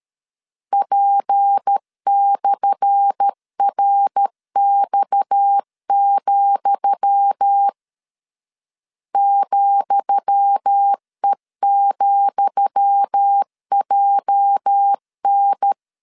Morse code, heard as a keyed tone, about 13 wpm, 800 Hz. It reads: P/RX, ,E,JN
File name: LEZIONI / 10 SIGNS / 10D COMA 12wpm 2544